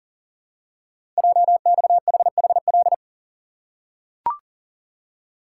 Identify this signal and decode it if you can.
Morse code, heard as a keyed tone, about 42 wpm, 700 Hz. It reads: JXHHL
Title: Code Groups - 5 Letters - 40wpm